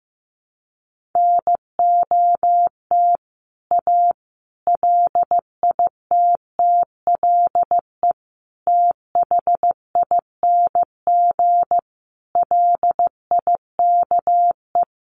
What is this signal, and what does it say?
Morse code, heard as a keyed tone, about 15 wpm, 700 Hz. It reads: NOT A LITTLE THING LIKE
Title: A Sound Of Thunder 15wpm